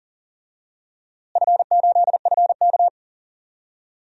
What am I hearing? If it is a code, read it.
Morse code, heard as a keyed tone, about 40 wpm, 700 Hz. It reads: F8FK
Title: DXCC-100 Call Signs - 4 Characters - Repeated 2x - 40wpm